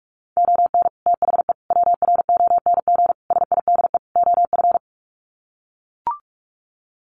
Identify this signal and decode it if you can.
Morse code, heard as a keyed tone, about 44 wpm, 700 Hz. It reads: ON THE WRONG SIDE OF
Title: Sets of 5 Words Encore 45wpm